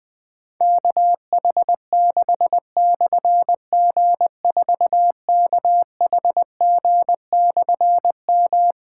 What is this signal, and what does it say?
Morse code, heard as a keyed tone, about 21 wpm, 700 Hz. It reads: KH6/G4KHG/M
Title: Difficult Call Signs 20wpm